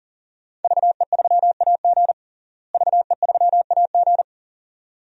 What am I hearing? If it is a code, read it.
Morse code, heard as a keyed tone, about 40 wpm, 700 Hz. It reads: VE3AG VE3AG
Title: DXCC-100 Call Signs - 5 Characters - Repeated 2x - 40wpm